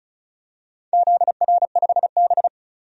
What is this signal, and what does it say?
Morse code, heard as a keyed tone, about 34 wpm, 700 Hz. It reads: ZR5B
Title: DXCC-100 Call Signs - 4 Characters - 35wpm